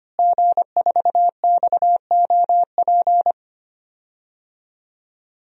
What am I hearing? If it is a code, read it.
Morse code, heard as a keyed tone, about 25 wpm, 700 Hz. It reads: G4XOP
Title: DXCC-100 Call Signs - 5 Characters - Repeated 2x - 25wpm